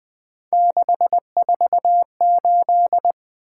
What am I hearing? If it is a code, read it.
Morse code, heard as a keyed tone, about 20 wpm, 700 Hz. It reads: 648
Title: Three Letter-Number Combinations - 20wpm